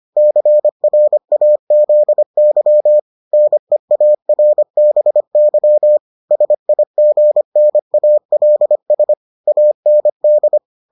Morse code, heard as a keyed tone, about 25 wpm, 600 Hz. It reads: CRAZY NEARBY SIGNALS AND